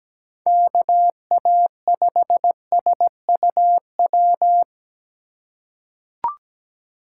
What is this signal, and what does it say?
Morse code, heard as a keyed tone, about 17 wpm, 700 Hz. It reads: KA5SUW
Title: USA Call Signs - 3x Character Spacing - 17wpm